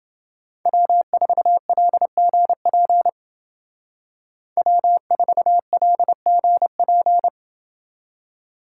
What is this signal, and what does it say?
Morse code, heard as a keyed tone, about 27 wpm, 700 Hz. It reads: W4LGP W4LGP